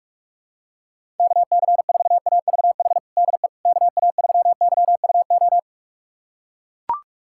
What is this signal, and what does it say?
Morse code, heard as a keyed tone, about 42 wpm, 700 Hz. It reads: KC4AVH DE KA3YUO
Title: USA Callsign DE Callsign - 6 Characters - 45wpm